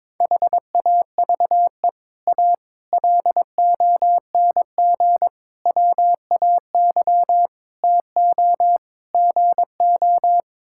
Morse code, heard as a keyed tone, about 22 wpm, 700 Hz. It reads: HAVE A LONG WAY TO GO
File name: practice / 22wpm / Sets of 6 Words - 22wpm